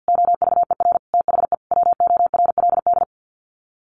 Morse code, heard as a keyed tone, about 50 wpm, 700 Hz. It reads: OVER THE WORLD
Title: Sets of 4 Words 50wpm